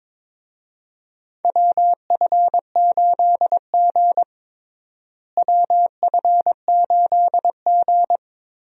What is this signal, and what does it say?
Morse code, heard as a keyed tone, about 22 wpm, 700 Hz. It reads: WF8G WF8G
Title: USA Call Signs - 4 Characters - Repeated 3x - 22wpm